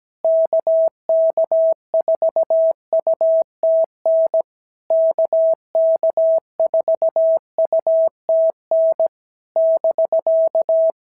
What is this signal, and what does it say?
Morse code, heard as a keyed tone, about 17 wpm, 650 Hz. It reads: KK4UTN KK4UTN <BK>
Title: SOTA QSO Practice 17wpm